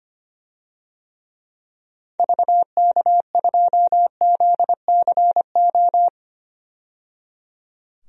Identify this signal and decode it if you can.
Morse code, heard as a keyed tone, about 25 wpm, 700 Hz. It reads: VK2ZCO
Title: DXCC-100 Call Signs - 6 Characters - 25wpm